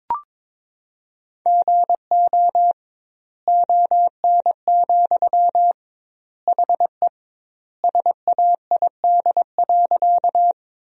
Morse code, heard as a keyed tone, about 22 wpm, 700 Hz. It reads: GO ON, HE SAID.
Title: 2x Word-Spacing - Sentences from Top 100 Words - 22wpm